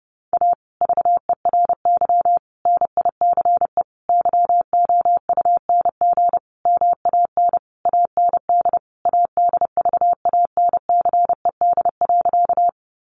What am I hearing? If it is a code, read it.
Morse code, heard as a keyed tone, about 30 wpm, 700 Hz. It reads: A VERY NICE YOUNG MAN AND ADVANCED.
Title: Sentences from Top 1200 Words 30wpm